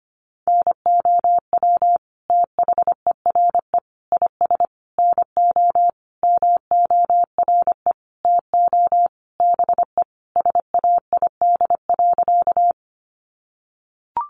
Morse code, heard as a keyed tone, about 25 wpm, 700 Hz. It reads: NOW THERE IS NO MORE TO BE SAID.
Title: Sentences from Top 100 Words 25wpm